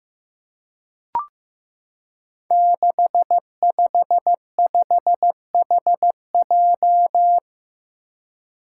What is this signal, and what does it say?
Morse code, heard as a keyed tone, about 15 wpm, 700 Hz. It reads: 655HJ
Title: Five Challenging Letter-Number Combinations 15wpm